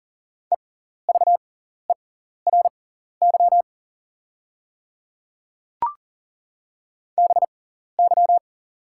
Morse code, heard as a keyed tone, about 39 wpm, spaced as for 14 wpm, 690 Hz. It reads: EVERY BY
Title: 6x Character Spacing - Top 300 Words - 40wpm